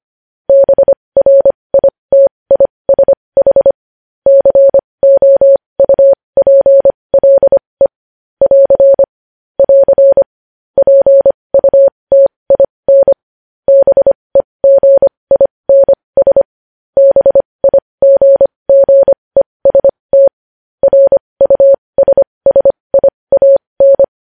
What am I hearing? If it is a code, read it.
Morse code, heard as a keyed tone, about 25 wpm, spaced as for 20 wpm, 550 Hz. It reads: BRITISH COUPLE <AR> <AR> PUTIN BEGINS BIGGEST RUSSIAN
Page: news-headlines-in-morse-code-at-20-wpm